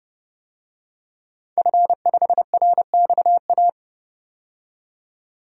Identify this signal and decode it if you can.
Morse code, heard as a keyed tone, about 29 wpm, 700 Hz. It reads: F5RXA
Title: DXCC-100 Call Signs - 5 Characters - 30wpm